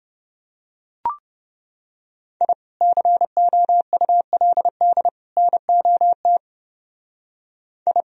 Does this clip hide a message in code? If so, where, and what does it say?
Yes: Morse code from 2.4 to 8.0 s, heard as a keyed tone, about 28 wpm, 700 Hz. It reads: I COULD NOT I